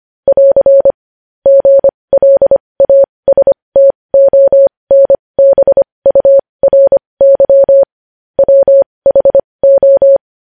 Morse code, heard as a keyed tone, about 25 wpm, spaced as for 20 wpm, 550 Hz. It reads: <AR> GLASTONBURY WHO
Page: News Headlines in Morse Code at 20 WPM